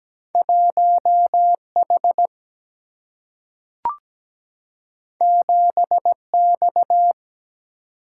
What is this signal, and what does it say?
Morse code, heard as a keyed tone, about 17 wpm, 700 Hz. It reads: 1H 7X
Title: Single-Letter Single-Number 17wpm